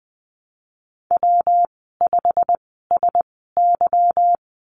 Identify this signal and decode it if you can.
Morse code, heard as a keyed tone, about 20 wpm, spaced as for 15 wpm, 700 Hz. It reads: W5SY
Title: 154 Lesson 25g - Letter p - Call Signs 20wpm